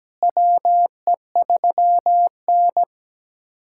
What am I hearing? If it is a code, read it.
Morse code, heard as a keyed tone, about 17 wpm, 700 Hz. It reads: WE3N
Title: USA Callsign DE Callsign - 4 Characters - 17wpm